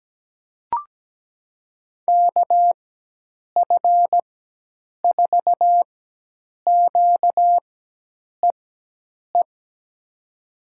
Morse code, heard as a keyed tone, about 17 wpm, spaced as for 8 wpm, 700 Hz. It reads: KF4QEE
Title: USA Call Signs - 6 Characters - 4x Character Spacing - 17wpm